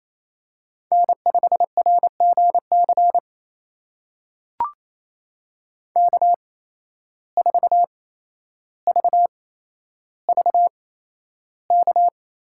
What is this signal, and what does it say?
Morse code, heard as a keyed tone, about 28 wpm, spaced as for 12 wpm, 700 Hz. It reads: N5RGC K4VVK
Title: USA Call Signs - 5 Characters - 8x Character Spacing - 28wpm